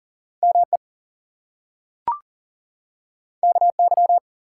Morse code, heard as a keyed tone, about 40 wpm, 700 Hz. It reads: ME KY
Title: US States and Canadian Provinces 40wpm